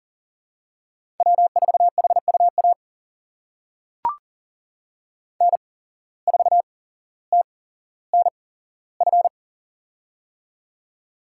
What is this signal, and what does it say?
Morse code, heard as a keyed tone, about 38 wpm, spaced as for 17 wpm, 700 Hz. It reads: W4HUA N4TNF
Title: USA Call Signs - 5 Characters - 8x Character Spacing - 40wpm